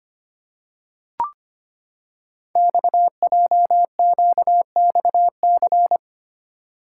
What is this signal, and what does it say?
Morse code, heard as a keyed tone, about 25 wpm, 700 Hz. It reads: XJQXC
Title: Code Groups - 5 Letters - 25wpm